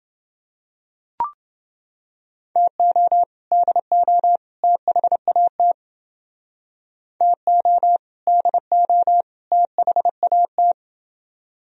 Morse code, heard as a keyed tone, about 29 wpm, 700 Hz. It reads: TO DO THAT TO DO THAT